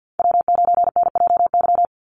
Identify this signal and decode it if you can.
Morse code, heard as a keyed tone, about 50 wpm, 700 Hz. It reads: W9NJY
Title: Call Signs - 50wpm